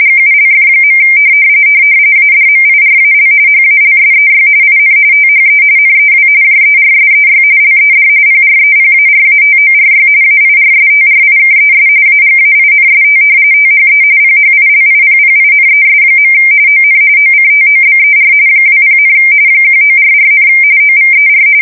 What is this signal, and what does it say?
RTTY, about 45.45 baud. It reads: GULF. ADMIRAL HALSEY IN THE NEW JERSEY LED PREPARATORY AIRSTRIKES AGAINST THE PHILIPPINES. ON OCTOBER 20, 1944, OVER
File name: fsk170 / battleshipnj